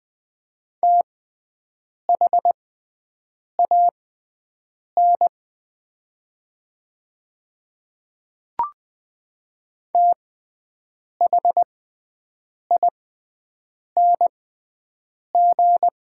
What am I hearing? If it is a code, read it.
Morse code, heard as a keyed tone, about 20 wpm, spaced as for 7 wpm, 700 Hz. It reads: THAN THING
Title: 6x Character Spacing - Top 300 Words - 20wpm